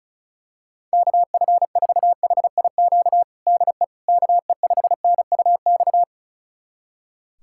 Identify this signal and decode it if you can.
Morse code, heard as a keyed tone, about 36 wpm, 700 Hz. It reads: KF4HIQ DE KE5NUX
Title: USA Callsign DE Callsign 35wpm